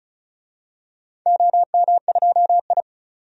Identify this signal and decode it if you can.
Morse code, heard as a keyed tone, about 36 wpm, 700 Hz. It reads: OM2I
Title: DXCC-100 Call Signs - 4 Characters - 35wpm